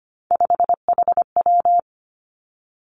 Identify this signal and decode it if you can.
Morse code, heard as a keyed tone, about 25 wpm, 700 Hz. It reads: 5HW
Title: Three Challenging Letter-Number Combinations 25wpm